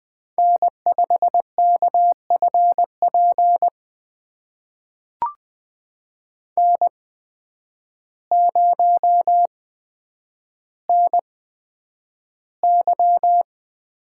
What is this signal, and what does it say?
Morse code, heard as a keyed tone, about 20 wpm, spaced as for 9 wpm, 700 Hz. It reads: N5KFP N0NY
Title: USA Call Signs - 5 Characters - 8x Character Spacing - 20wpm